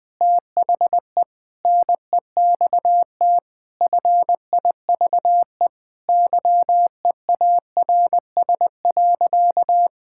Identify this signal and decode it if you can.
Morse code, heard as a keyed tone, about 21 wpm, 700 Hz. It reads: THE NEXT FIVE YEARS.
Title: Sentences with 5-Word Phrases - Spoken Beginning - 20wpm